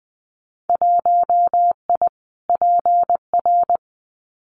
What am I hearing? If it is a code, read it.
Morse code, heard as a keyed tone, about 20 wpm, 700 Hz. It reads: 1I PR
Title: Winter Field Day Exchanges Encore 20wpm